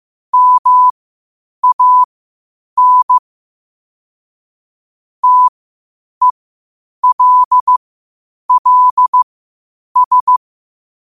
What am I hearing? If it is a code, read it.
Morse code, heard as a keyed tone, about 15 wpm, spaced as for 8 wpm, 1000 Hz. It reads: MAN TELLS
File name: qotdfiles / 8wpm